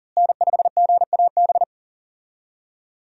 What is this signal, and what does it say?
Morse code, heard as a keyed tone, about 40 wpm, 700 Hz. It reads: N5GAB
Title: USA Call Signs - 5 Characters - Repeated 2x - 40wpm